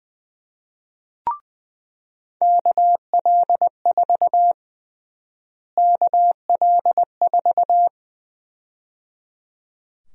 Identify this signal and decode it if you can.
Morse code, heard as a keyed tone, about 20 wpm, 700 Hz. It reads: KL4 KL4
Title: USA Call Sign Prefixes - Repeated 2x - 20wpm